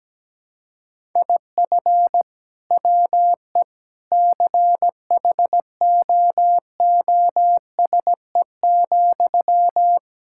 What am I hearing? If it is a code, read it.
Morse code, heard as a keyed tone, about 17 wpm, 700 Hz. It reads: IF WE CHOOSE,
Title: Sentences from Top 1000 Words - Review Part 6 - 17wpm